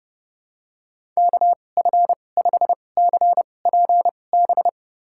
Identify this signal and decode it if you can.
Morse code, heard as a keyed tone, about 32 wpm, spaced as for 22 wpm, 700 Hz. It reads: KF5CPB
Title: USA Call Signs - 6 Characters - 2x Character Spacing - 30wpm